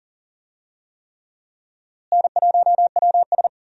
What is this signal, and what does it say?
Morse code, heard as a keyed tone, about 40 wpm, 700 Hz. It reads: N1WS
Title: USA Call Signs - 4 Characters - Repeated 3x - 40wpm